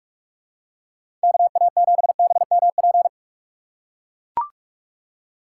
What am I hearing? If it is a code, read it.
Morse code, heard as a keyed tone, about 44 wpm, 700 Hz. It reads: KA7BMP
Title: USA Call Signs - 6 Characters - 6x Character Spacing - 45wpm